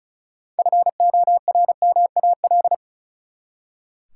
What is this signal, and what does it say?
Morse code, heard as a keyed tone, about 36 wpm, 700 Hz. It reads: FORMAL
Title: Six Letter Words - 35wpm